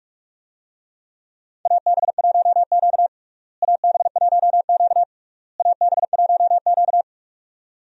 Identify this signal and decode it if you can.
Morse code, heard as a keyed tone, about 47 wpm, 700 Hz. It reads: AB1Q AB1Q AB1Q
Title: USA Call Signs - 4 Characters - Repeated 3x - 45wpm